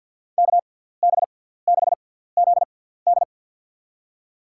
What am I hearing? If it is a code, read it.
Morse code, heard as a keyed tone, about 53 wpm, spaced as for 17 wpm, 700 Hz. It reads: KB6ZD
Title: USA Call Signs - 5 Characters - 6x Character Spacing - 50wpm